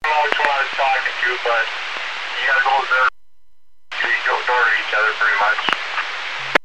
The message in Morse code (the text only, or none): none